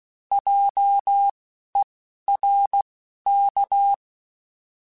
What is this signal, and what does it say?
Morse code, heard as a keyed tone, about 16 wpm, spaced as for 11 wpm, 800 Hz. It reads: JERK
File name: Alphabetical / Alphabetical mp3s / Js @10